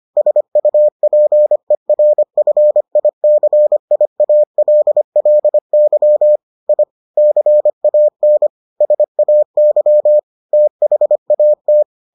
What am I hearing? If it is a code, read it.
Morse code, heard as a keyed tone, about 25 wpm, 600 Hz. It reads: SUPERFICIALLY I CAN SAY THAT